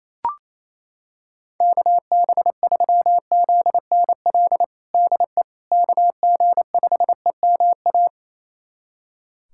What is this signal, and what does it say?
Morse code, heard as a keyed tone, about 28 wpm, 700 Hz. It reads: KB3ZNL DE KG5EMA
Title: USA Callsign DE Callsign - 6 Characters - 28wpm